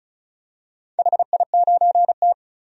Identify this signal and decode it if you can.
Morse code, heard as a keyed tone, about 36 wpm, 700 Hz. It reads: HI9T
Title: DXCC-100 Call Signs - 4 Characters - 35wpm